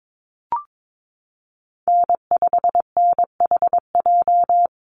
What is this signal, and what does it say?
Morse code, heard as a keyed tone, about 22 wpm, 700 Hz. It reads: N5NHJ
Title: Call Signs Repeated 2x - 22wpm